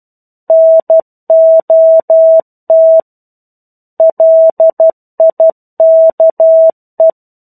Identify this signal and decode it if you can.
Morse code, heard as a keyed tone, about 12 wpm, 650 Hz. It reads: NOT LIKE
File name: 01-Chapter-01 12 WPM